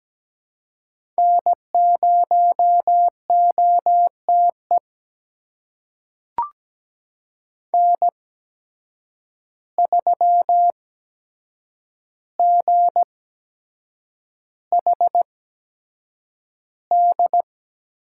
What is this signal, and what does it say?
Morse code, heard as a keyed tone, about 17 wpm, spaced as for 7 wpm, 700 Hz. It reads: N0OTE N3GHD
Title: USA Call Signs - 5 Characters - 8x Character Spacing - 17wpm